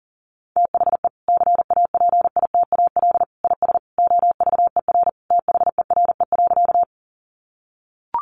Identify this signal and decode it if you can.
Morse code, heard as a keyed tone, about 40 wpm, 700 Hz. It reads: THE CAPITAL IS OVER THERE.
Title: Sentences from Top 1000 Words 40wpm